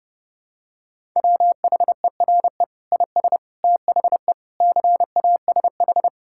Morse code, heard as a keyed tone, about 30 wpm, 700 Hz. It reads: WHERE IS THE CASH
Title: Sentences from Top 1000 Words - Review Part 5 - 30wpm